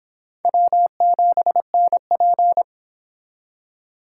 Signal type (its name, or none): Morse code